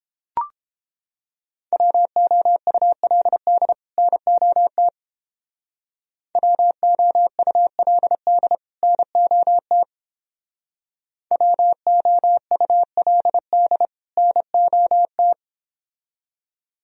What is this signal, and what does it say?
Morse code, heard as a keyed tone, about 29 wpm, 700 Hz. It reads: WOULD NOT WOULD NOT WOULD NOT